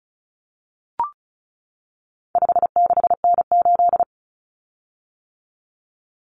Morse code, heard as a keyed tone, about 34 wpm, 700 Hz. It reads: 56N8
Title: Common Cut Numbers - 4-Digit 35wpm